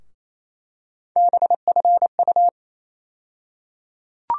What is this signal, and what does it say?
Morse code, heard as a keyed tone, about 28 wpm, 700 Hz. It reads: BFU
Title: USA Call Sign Postfixes - 3 Letters 28wpm